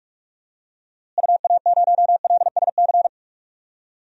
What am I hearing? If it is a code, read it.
Morse code, heard as a keyed tone, about 46 wpm, 700 Hz. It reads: UA0LSC
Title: DXCC-100 Call Signs - 6 Characters - Repeated 3x - 45wpm